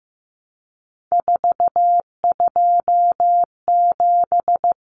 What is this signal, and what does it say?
Morse code, heard as a keyed tone, about 15 wpm, 700 Hz. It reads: 427
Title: Common Cut Numbers - 3-Digit 15wpm